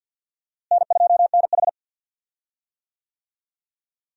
Morse code, heard as a keyed tone, about 50 wpm, 700 Hz. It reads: N2NH
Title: USA Call Signs - 4 Characters - Repeated 2x - 50wpm